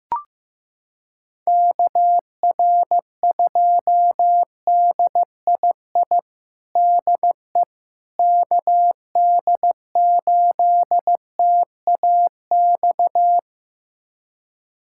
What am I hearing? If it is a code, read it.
Morse code, heard as a keyed tone, about 15 wpm, 700 Hz. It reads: KR2DII DE KD8TAX
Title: USA Callsign DE Callsign 15wpm